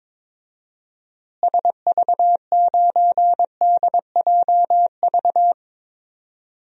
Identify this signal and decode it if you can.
Morse code, heard as a keyed tone, about 22 wpm, 700 Hz. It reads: SV9DJV
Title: DXCC-100 Call Signs - 6 Characters - Repeated 2x - 22wpm